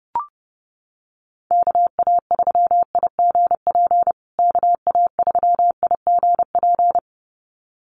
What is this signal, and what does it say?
Morse code, heard as a keyed tone, about 28 wpm, 700 Hz. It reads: KA3IGP KA3IGP
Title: USA Call Signs Repeated 2x 30wpm